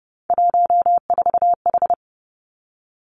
Morse code, heard as a keyed tone, about 32 wpm, 700 Hz. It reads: 14H
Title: Three Challenging Letter-Number Combinations 30wpm